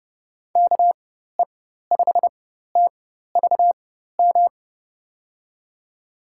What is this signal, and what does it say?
Morse code, heard as a keyed tone, about 30 wpm, spaced as for 14 wpm, 700 Hz. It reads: KE5TVM
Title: USA Call Signs - 6 Characters - 4x Character Spacing - 30wpm